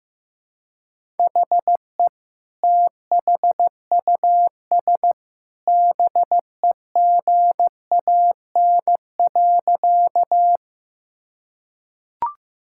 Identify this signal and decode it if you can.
Morse code, heard as a keyed tone, about 15 wpm, 700 Hz. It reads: HE THUS BEGAN.
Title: Sentences from Top 1000 Words - Review Part 5 - 15wpm